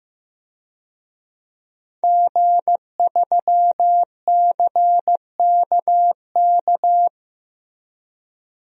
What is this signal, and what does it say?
Morse code, heard as a keyed tone, about 15 wpm, 700 Hz. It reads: G3CKK
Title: DXCC-100 Call Signs - 5 Characters - Repeated 2x - 15wpm